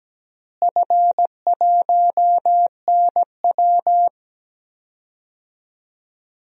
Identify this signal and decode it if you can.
Morse code, heard as a keyed tone, about 17 wpm, 700 Hz. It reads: F1NW
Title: DXCC-100 Call Signs - 4 Characters - Repeated 2x - 17wpm